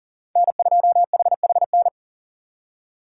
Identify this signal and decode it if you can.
Morse code, heard as a keyed tone, about 48 wpm, 700 Hz. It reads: N2HHN
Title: Speed Racing - USA 1x3 Call Signs 40wpm